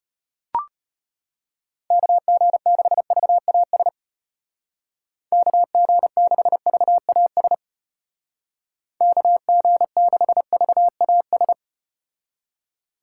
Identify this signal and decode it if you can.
Morse code, heard as a keyed tone, about 34 wpm, 700 Hz. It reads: KG6VAS KG6VAS KG6VAS